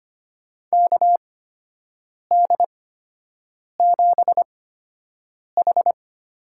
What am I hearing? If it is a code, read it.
Morse code, heard as a keyed tone, about 25 wpm, spaced as for 7 wpm, 700 Hz. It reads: KD7H
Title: USA Call Signs - 6 Characters - 8x Character Spacing - 25wpm